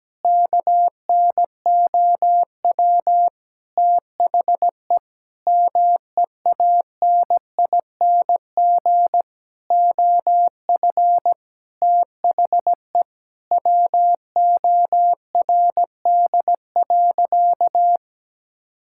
Morse code, heard as a keyed tone, about 17 wpm, 700 Hz. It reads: KNOW THE MEANING OF THE WORD.
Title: Sentences from Top 1000 Words - Review Part 6 - 17wpm